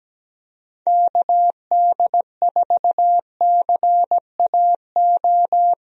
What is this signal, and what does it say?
Morse code, heard as a keyed tone, about 17 wpm, 700 Hz. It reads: KD4CAO
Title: USA Call Signs - 6 Characters - Repeated 2x - 17wpm